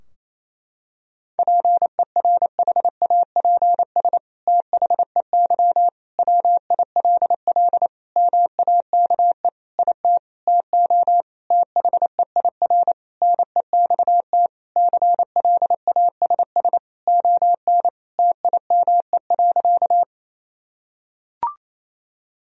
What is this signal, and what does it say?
Morse code, heard as a keyed tone, about 28 wpm, 700 Hz. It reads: PERHAPS THEY WILL MAKE IT TO THEIR NEXT CLASS ON TIME.
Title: Sentences from Top 1000 Words - Review Part 2 - 28wpm